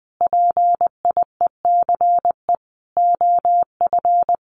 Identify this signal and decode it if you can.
Morse code, heard as a keyed tone, about 19 wpm, 700 Hz. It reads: PIECE OF
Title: Sentences from Top 1500 Words 20wpm